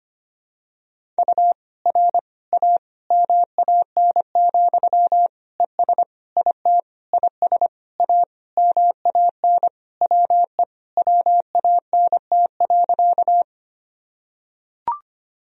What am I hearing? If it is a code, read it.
Morse code, heard as a keyed tone, about 25 wpm, 700 Hz. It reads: U R A MAN, ES IT IS A MAN WE WANT.
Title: Sentences from Top 200 Words and Abbreviations 25wpm